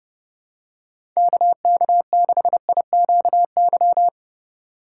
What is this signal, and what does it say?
Morse code, heard as a keyed tone, about 30 wpm, 700 Hz. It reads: KK6IQY
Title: USA Call Signs - 6 Characters - 2x Character Spacing - 30wpm